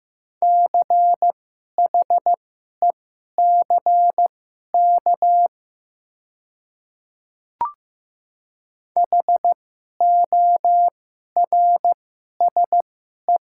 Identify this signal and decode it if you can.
Morse code, heard as a keyed tone, about 15 wpm, spaced as for 11 wpm, 700 Hz. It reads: CHECK HORSE
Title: 2x Character Spacing - Top 500 Words - 15wpm